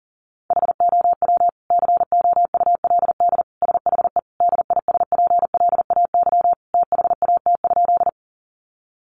Morse code, heard as a keyed tone, about 37 wpm, 700 Hz. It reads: HOW COULD SHE DISPLAY THAT?
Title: Sentences from Top 1200 Words 40wpm